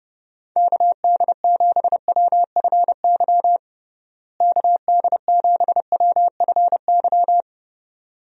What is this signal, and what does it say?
Morse code, heard as a keyed tone, about 30 wpm, 700 Hz. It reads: KD7WFY KD7WFY
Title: USA Call Signs - 6 Characters - Repeated 2x - 30wpm